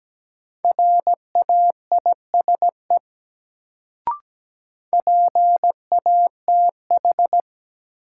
Morse code, heard as a keyed tone, about 17 wpm, 700 Hz. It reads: RAISE PATH
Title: Sentences from Top 1400 Words - New Word Drill Alt - 17wpm